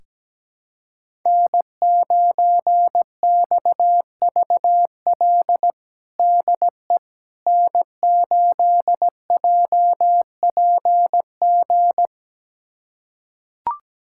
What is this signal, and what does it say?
Morse code, heard as a keyed tone, about 17 wpm, 700 Hz. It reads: N9XVL DE N8JPG
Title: USA Callsign DE Callsign - 5 Characters - 17wpm